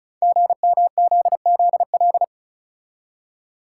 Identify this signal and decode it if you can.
Morse code, heard as a keyed tone, about 36 wpm, 700 Hz. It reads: GMZZL
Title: Code Groups - 5 Letters - 35wpm